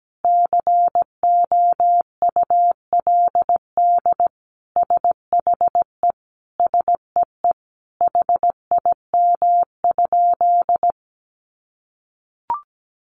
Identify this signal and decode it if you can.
Morse code, heard as a keyed tone, about 17 wpm, 700 Hz. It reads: COULD SHE SEE HIM?
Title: Sentences from Top 100 Words 17wpm